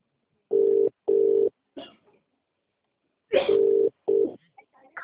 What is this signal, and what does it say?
Ringback tone (UK), 400 + 450 Hz.